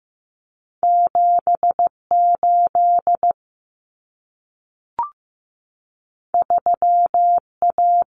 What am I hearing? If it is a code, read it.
Morse code, heard as a keyed tone, about 15 wpm, 700 Hz. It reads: 78 3A
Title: Common Cut Numbers - 2-Digit 15wpm